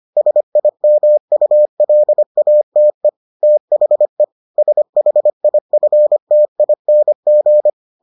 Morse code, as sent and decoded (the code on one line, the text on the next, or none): ... .. -- ..- .-.. .- - . / - .... / ... .... .. ..-. - .. -. --.
SIMULATE THE SHIFTING